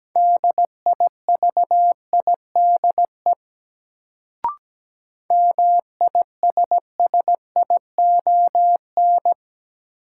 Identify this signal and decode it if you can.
Morse code, heard as a keyed tone, about 17 wpm, 700 Hz. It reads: DIVIDE MISSION
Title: Sentences from Top 1600 Words - New Word Drill - 17wpm